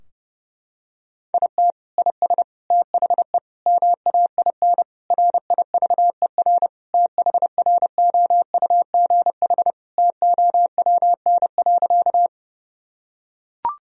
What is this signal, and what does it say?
Morse code, heard as a keyed tone, about 30 wpm, 700 Hz. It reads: IT IS THE MAIN RIVER THROUGH TOWN.
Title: Sentences from Top 1000 Words - Review Part 2 - 30wpm